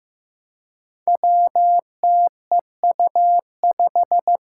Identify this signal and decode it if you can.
Morse code, heard as a keyed tone, about 15 wpm, 700 Hz. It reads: WTEU5
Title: Five Challenging Letter-Number Combinations 15wpm